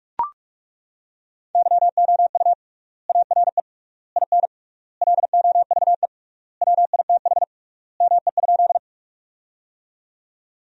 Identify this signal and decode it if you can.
Morse code, heard as a keyed tone, about 41 wpm, 700 Hz. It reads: YOU ARE IN LOVE WITH ME?
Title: 3x Word-Spacing - Sentences from Top 400 Words - 45wpm